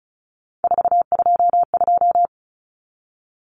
Morse code, heard as a keyed tone, about 34 wpm, 700 Hz. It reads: 422
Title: Numbers - 3-Digit 35wpm